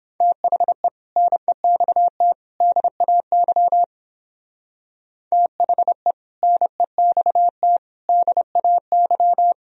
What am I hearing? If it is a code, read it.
Morse code, heard as a keyed tone, about 31 wpm, 700 Hz. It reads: THE NEXT DAY THE NEXT DAY